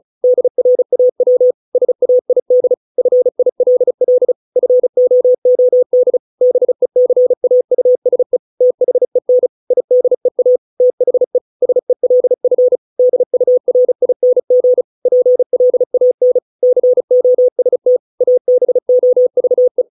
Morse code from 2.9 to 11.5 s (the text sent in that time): FILL FOOD BECAUSE THEN IDEA THE